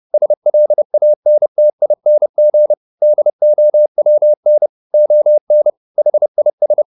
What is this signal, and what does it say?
Morse code, heard as a keyed tone, about 30 wpm, 600 Hz. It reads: SLANTING DOWN ON HIS
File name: the old man and the sea / the old man and the sea 1 30wpm